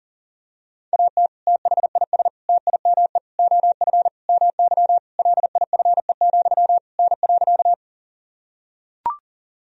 Morse code, heard as a keyed tone, about 40 wpm, 700 Hz. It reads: AT THIS TIME OF MY LIFE, N.
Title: Sentences from Top 300 Words and Abbreviations 40wpm